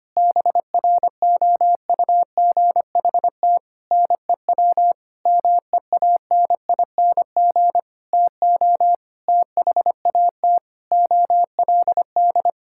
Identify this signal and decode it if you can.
Morse code, heard as a keyed tone, about 25 wpm, 700 Hz. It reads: BROUGHT NEW MEANING TO THAT OLD
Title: Sentences from Top 1600 Words 25wpm